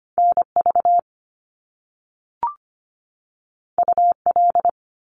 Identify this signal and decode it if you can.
Morse code, heard as a keyed tone, about 25 wpm, 700 Hz. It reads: NV UL
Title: Top Two Letter Combinations in Words 25wpm